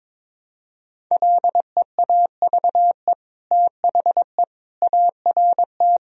Morse code, heard as a keyed tone, about 22 wpm, 700 Hz. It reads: LEAVE THE ART
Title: Sentences from Top 1000 Words - Review Part 3 - 22wpm